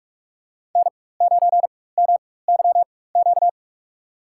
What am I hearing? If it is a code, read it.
Morse code, heard as a keyed tone, about 46 wpm, spaced as for 21 wpm, 700 Hz. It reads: N9MYQ
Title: USA Call Signs - 5 Characters - 4x Character Spacing - 45wpm